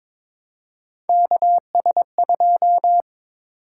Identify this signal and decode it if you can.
Morse code, heard as a keyed tone, about 22 wpm, 700 Hz. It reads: KS2
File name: practice / 22wpm / Three Letter-Number Combinations - 22wpm